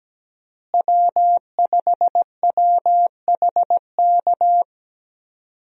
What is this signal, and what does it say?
Morse code, heard as a keyed tone, about 17 wpm, 700 Hz. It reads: W5WHK
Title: USA Call Signs - 5 Characters - Repeated 3x - 17wpm